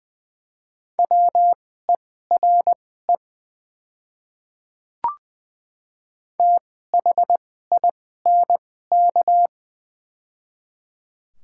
Morse code, heard as a keyed tone, about 20 wpm, spaced as for 14 wpm, 700 Hz. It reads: WERE THINK
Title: 2x Character Spacing - Top 200 Words - 20wpm